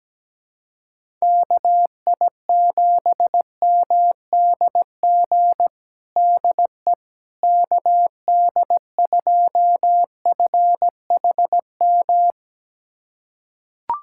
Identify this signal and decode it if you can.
Morse code, heard as a keyed tone, about 17 wpm, 700 Hz. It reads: KI7MDG DE KD2FHM
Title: USA Callsign DE Callsign 17wpm